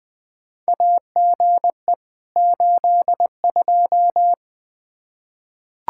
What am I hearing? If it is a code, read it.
Morse code, heard as a keyed tone, about 20 wpm, 700 Hz. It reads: AGE 82
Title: K1USN SST QSO Practice 20wpm